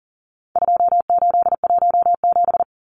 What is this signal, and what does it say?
Morse code, heard as a keyed tone, about 40 wpm, 700 Hz. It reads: 2817
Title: Numbers - 4-Digit 40wpm